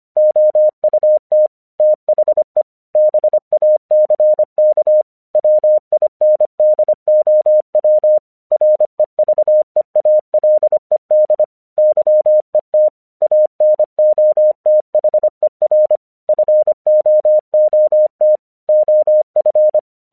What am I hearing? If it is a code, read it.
Morse code, heard as a keyed tone, about 25 wpm, 610 Hz. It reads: OUT THE BACK WINDOW REVEALED YET ANOTHER FOOT OF
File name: English 25wpm